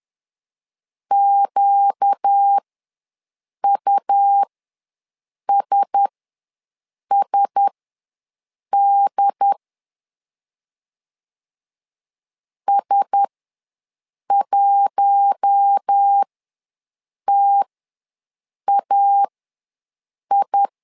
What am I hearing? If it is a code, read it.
Morse code, heard as a keyed tone, about 10 wpm, spaced as for 6 wpm, 800 Hz. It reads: QUSSD S1TAI
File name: LEZIONI / 4 Q1S ws 10wpm 2208